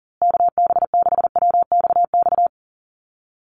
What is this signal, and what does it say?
Morse code, heard as a keyed tone, about 40 wpm, 700 Hz. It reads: KB6WXX
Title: USA Call Signs Repeated 2x 40wpm